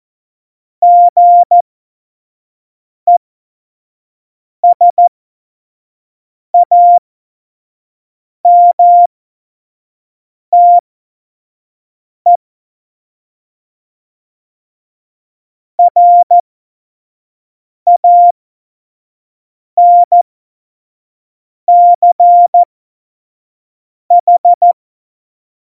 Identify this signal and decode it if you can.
Morse code, heard as a keyed tone, about 14 wpm, spaced as for 5 wpm, 700 Hz. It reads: GESAMTE RANCH